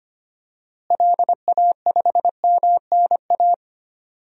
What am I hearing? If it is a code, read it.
Morse code, heard as a keyed tone, about 25 wpm, 700 Hz. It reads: LA5MNA
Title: DXCC-100 Call Signs - 6 Characters - Repeated 2x - 25wpm